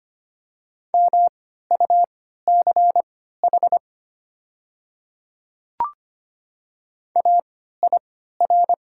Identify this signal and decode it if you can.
Morse code, heard as a keyed tone, about 25 wpm, spaced as for 14 wpm, 700 Hz. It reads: MUCH AIR
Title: 3x Character Spacing - Top 200 Words - 25wpm